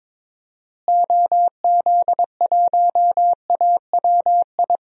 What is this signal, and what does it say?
Morse code, heard as a keyed tone, about 22 wpm, 700 Hz. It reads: OZ1AWI
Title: DXCC-100 Call Signs - 6 Characters - Repeated 3x - 22wpm